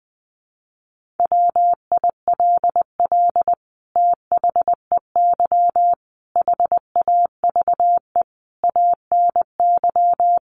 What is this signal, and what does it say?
Morse code, heard as a keyed tone, about 20 wpm, 700 Hz. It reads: WILL THEY HAVE ANY
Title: Sentences from Top 1500 Words 20wpm